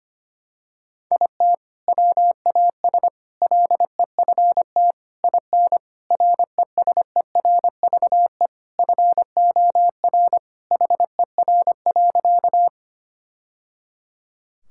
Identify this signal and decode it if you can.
Morse code, heard as a keyed tone, about 25 wpm, 700 Hz. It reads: IT WAS LEFT IN RESERVE FOR HER.
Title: Sentences from Top 1000 Words - Review Part 6 - 25wpm